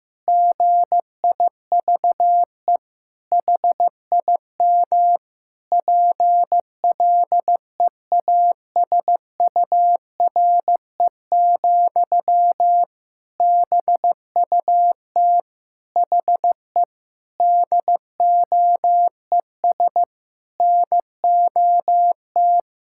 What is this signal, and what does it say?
Morse code, heard as a keyed tone, about 15 wpm, 700 Hz. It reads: GIVE HIM PLEASURE, BUT HE DOES NOT
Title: Sentences from Top 1600 Words 15wpm